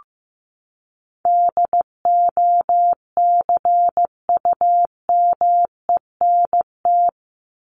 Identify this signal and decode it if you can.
Morse code, heard as a keyed tone, about 15 wpm, 700 Hz. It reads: DOCUMENT
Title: Top 1751-2000 Words 15wpm